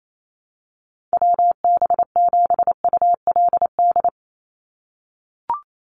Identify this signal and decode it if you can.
Morse code, heard as a keyed tone, about 28 wpm, 700 Hz. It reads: WB7ULD
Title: NA SOTA Chaser Callsigns 28wpm